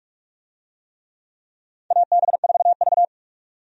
Morse code, heard as a keyed tone, about 41 wpm, 700 Hz. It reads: AB4V
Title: USA Call Signs - 4 Characters - 45wpm